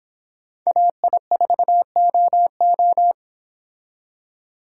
Morse code, heard as a keyed tone, about 23 wpm, 700 Hz. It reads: AI4OO